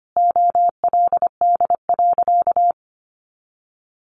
Morse code, heard as a keyed tone, about 25 wpm, 700 Hz. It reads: OLD.
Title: Sentences from Top 1400 Words 25wpm